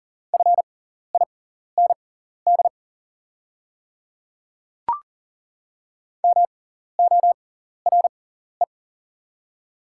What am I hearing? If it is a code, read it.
Morse code, heard as a keyed tone, about 40 wpm, spaced as for 14 wpm, 690 Hz. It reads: FIND MORE
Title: 6x Character Spacing - Top 100 Words - 40wpm